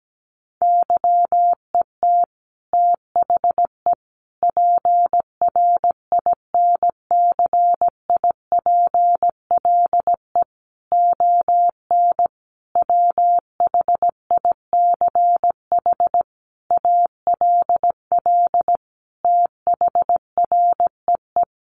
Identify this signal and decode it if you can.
Morse code, heard as a keyed tone, about 17 wpm, 700 Hz. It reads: YET THE PRINCIPLE ON WHICH ALL THREE
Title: Sentences from Top 1400 Words 17wpm